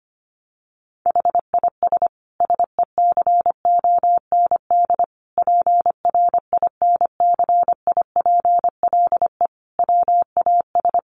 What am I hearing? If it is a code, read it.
Morse code, heard as a keyed tone, about 25 wpm, 700 Hz. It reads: HIS SECOND PRINCIPLE WAS
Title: Sentences from Top 1400 Words 25wpm